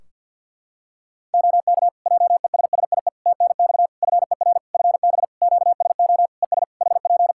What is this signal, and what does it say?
Morse code, heard as a keyed tone, about 51 wpm, 700 Hz. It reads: OK JESSIE TNX FER FB QSO ES HP
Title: Practice QSO - Exchange 4 - Ending - 50wpm